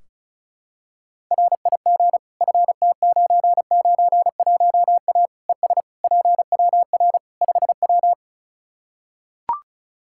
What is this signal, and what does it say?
Morse code, heard as a keyed tone, about 34 wpm, 700 Hz. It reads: RIG FT991A ES PWR 5W
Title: Practice QSO - Exchange 2 - Info - 35wpm